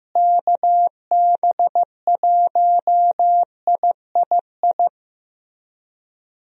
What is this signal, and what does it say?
Morse code, heard as a keyed tone, about 15 wpm, 700 Hz. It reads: KB1III
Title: USA Call Signs - 6 Characters - Repeated 3x - 15wpm